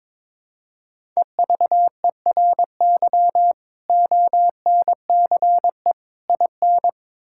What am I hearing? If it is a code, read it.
Morse code, heard as a keyed tone, about 21 wpm, 700 Hz. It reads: EVERY ONCE IN
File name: practice / 22wpm / Build A Phrase - 5-Word Phrases - 22wpm